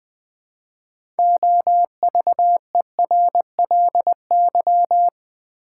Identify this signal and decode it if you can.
Morse code, heard as a keyed tone, about 20 wpm, 700 Hz. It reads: OVERLY
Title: 6 Letter Words - First 3 Letters Given - 20wpm